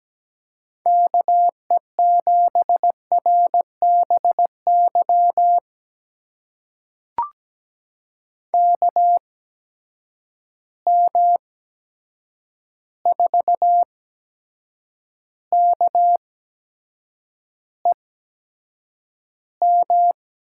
Morse code, heard as a keyed tone, about 17 wpm, spaced as for 7 wpm, 700 Hz. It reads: KE7RBY KM4KEM
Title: USA Call Signs - 6 Characters - 8x Character Spacing - 17wpm